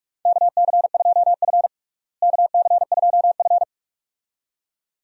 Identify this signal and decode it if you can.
Morse code, heard as a keyed tone, about 42 wpm, 700 Hz. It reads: KC2F KC2F
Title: USA Call Signs - 4 Characters - Repeated 2x - 45wpm